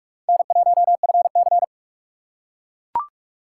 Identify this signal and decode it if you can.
Morse code, heard as a keyed tone, about 41 wpm, 700 Hz. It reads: N1FC